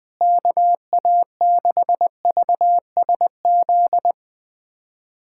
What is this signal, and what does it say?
Morse code, heard as a keyed tone, about 20 wpm, 700 Hz. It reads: KA6VSZ
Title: USA Callsign DE Callsign - 6 Characters - 20wpm